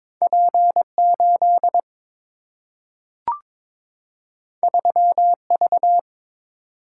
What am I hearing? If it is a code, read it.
Morse code, heard as a keyed tone, about 21 wpm, 700 Hz. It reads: P 8 3 V
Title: Single-Letter Single-Number 22wpm